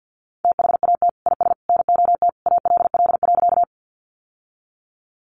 Morse code, heard as a keyed tone, about 50 wpm, 700 Hz. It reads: THAT IS NOT ALL.
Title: Sentences from Top 600 Words 50wpm